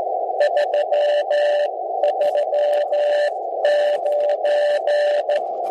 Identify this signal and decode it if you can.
Morse code, heard as a keyed tone, about 15 wpm, spaced as for 12 wpm, 590 Hz, with radio noise. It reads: 339
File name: ABI-SON-QSO-1903z- -7020r0